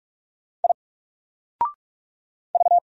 Morse code, heard as a keyed tone, about 44 wpm, 700 Hz. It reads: I V
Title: Single Letter-Number 45wpm